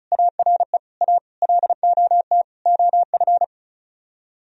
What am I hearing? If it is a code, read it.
Morse code, heard as a keyed tone, about 44 wpm, 700 Hz. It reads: ARE A LOT OF
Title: Speed-Racing - Sets of 5 Words 35wpm